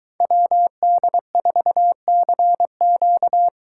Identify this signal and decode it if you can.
Morse code, heard as a keyed tone, about 23 wpm, 700 Hz. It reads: WD4CQ